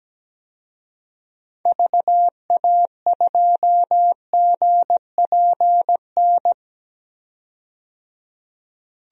Morse code, heard as a keyed tone, about 17 wpm, 700 Hz. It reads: VA2GPN
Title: DXCC-100 Call Signs - 6 Characters - 17wpm